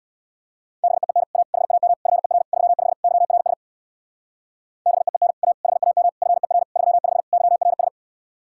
Morse code, heard as a keyed tone, about 62 wpm, 700 Hz. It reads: BEEN A HAM FER 35 YRS BEEN A HAM FER 35 YRS
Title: Speed-Racing Common QSO Phrases Encore - 50wpm